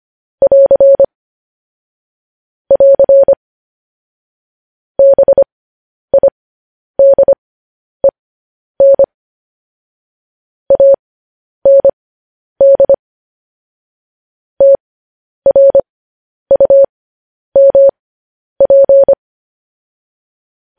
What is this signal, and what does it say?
Morse code, heard as a keyed tone, about 25 wpm, spaced as for 10 wpm, 550 Hz. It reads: <AR> <AR> BIDEN AND TRUMP